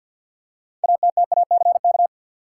Morse code, heard as a keyed tone, about 52 wpm, 700 Hz. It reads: ATTACK
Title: Six Letter Words - 50wpm